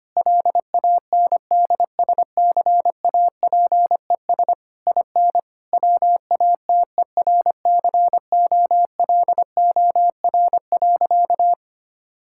Morse code, heard as a keyed tone, about 25 wpm, 700 Hz. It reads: LANDSCAPES IN WATERCOLOR.
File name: practice / 25wpm / Difficult Sentences with Hints - 25wpm